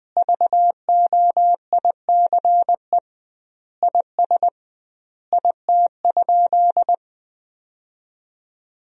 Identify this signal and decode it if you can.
Morse code, heard as a keyed tone, about 20 wpm, 700 Hz. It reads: VOICE IS IT?
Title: 2x Word-Spacing - Sentences from Top 500 Words - 20wpm